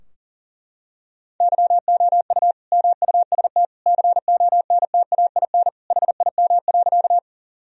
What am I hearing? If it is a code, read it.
Morse code, heard as a keyed tone, about 40 wpm, 700 Hz. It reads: YOU MUST CONTAIN HIM.
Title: Sentences from Top 1000 Words - Review Part 2 - 40wpm